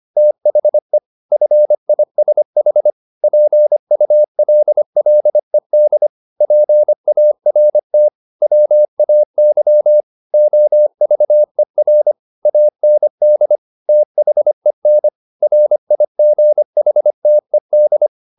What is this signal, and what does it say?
Morse code, heard as a keyed tone, about 25 wpm, 600 Hz. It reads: THE FISH PULLED PART WAY OVER AND THEN RIGHTED